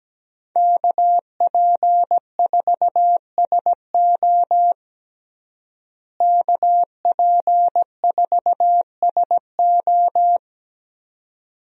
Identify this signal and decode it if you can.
Morse code, heard as a keyed tone, about 17 wpm, 700 Hz. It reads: KP4SO KP4SO
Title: DXCC-100 Call Signs - 5 Characters - Repeated 2x - 17wpm